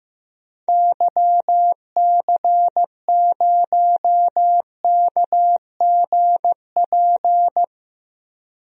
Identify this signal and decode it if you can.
Morse code, heard as a keyed tone, about 16 wpm, 700 Hz. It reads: YC0KGP
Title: DXCC-100 Call Signs - 6 Characters - 15wpm